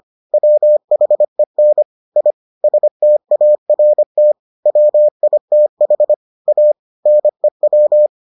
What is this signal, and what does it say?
Morse code, heard as a keyed tone, about 25 wpm, 600 Hz. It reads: WHEN I START WITH A NEW